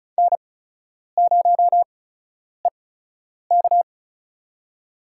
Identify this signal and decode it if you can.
Morse code, heard as a keyed tone, about 36 wpm, spaced as for 10 wpm, 700 Hz. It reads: N0EK
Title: USA Call Signs - 4 Characters - 8x Character Spacing - 35wpm